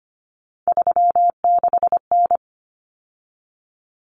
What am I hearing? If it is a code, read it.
Morse code, heard as a keyed tone, about 25 wpm, 700 Hz. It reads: 36N
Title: Common Cut Numbers - 3-Digit 25wpm